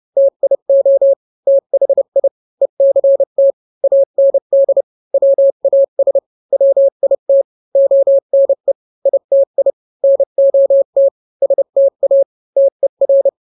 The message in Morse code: TIO THI ECT AND WAS WIT ONE ITI NOT STA TER